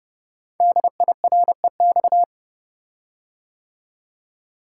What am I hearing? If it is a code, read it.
Morse code, heard as a keyed tone, about 30 wpm, 700 Hz. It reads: DIREX
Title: Code Groups - 5 Letters - 30wpm